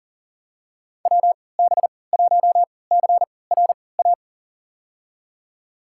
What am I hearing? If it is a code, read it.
Morse code, heard as a keyed tone, about 37 wpm, spaced as for 22 wpm, 700 Hz. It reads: WB1CRA
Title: USA Call Signs - 6 Characters - 3x Character Spacing - 40wpm